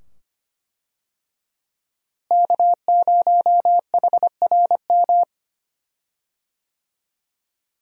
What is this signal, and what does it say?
Morse code, heard as a keyed tone, about 25 wpm, 700 Hz. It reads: K0HRM
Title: USA Call Signs - 5 Characters - Repeated 2x - 25wpm